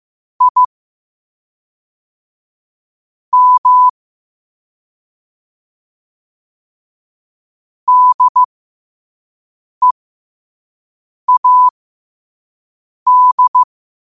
Morse code, heard as a keyed tone, about 15 wpm, spaced as for 5 wpm, 1000 Hz. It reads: I M DEAD
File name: qotdfiles / 5wpm